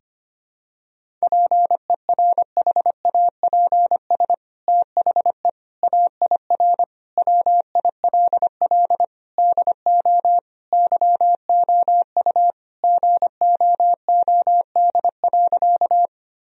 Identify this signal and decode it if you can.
Morse code, heard as a keyed tone, about 25 wpm, 700 Hz. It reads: PERHAPS THE AIR WILL DO YOU GOOD.
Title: Sentences from Top 1000 Words - Review Part 1 - 25wpm